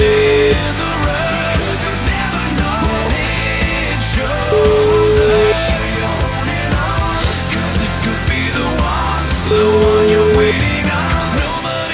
Ringback tone (Europe, 425 Hz), with noise.